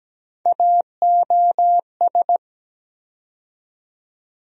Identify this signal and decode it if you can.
Morse code, heard as a keyed tone, about 17 wpm, 700 Hz. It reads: AOS
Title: USA Call Sign Postfixes - 3 Letters 17wpm